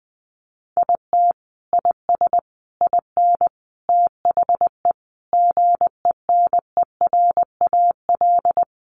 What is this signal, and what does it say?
Morse code, heard as a keyed tone, about 20 wpm, 700 Hz. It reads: IT IS IN THE GENERAL
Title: Sentences from Top 1200 Words 20wpm